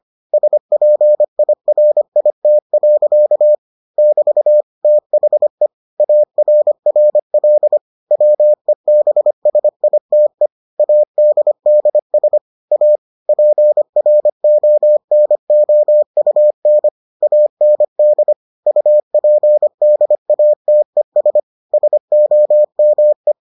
Morse code, heard as a keyed tone, about 25 wpm, 600 Hz. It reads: SPIRIT. <BT> THE ARRL WEBSITE ADDS A PRONOUN AND UPDATES SOME